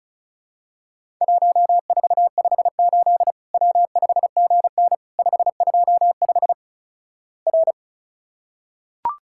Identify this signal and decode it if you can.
Morse code, heard as a keyed tone, about 36 wpm, 700 Hz. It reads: 1458 W5GN 525 R
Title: Worked All Europe DX Contest - QTC Exchange - 35wpm